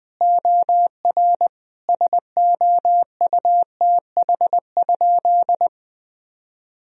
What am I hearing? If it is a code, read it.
Morse code, heard as a keyed tone, about 21 wpm, 700 Hz. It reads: OR SOUTH?
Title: Sentences from Top 1000 Words - Review Part 2 - 20wpm